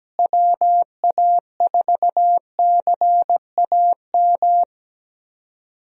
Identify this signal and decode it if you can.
Morse code, heard as a keyed tone, about 17 wpm, 700 Hz. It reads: WA4CAM
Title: USA Call Signs - 6 Characters - Repeated 2x - 17wpm